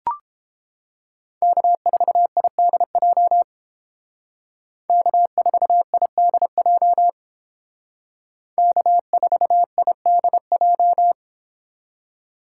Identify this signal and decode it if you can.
Morse code, heard as a keyed tone, about 30 wpm, 700 Hz. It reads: K4IDJ K4IDJ K4IDJ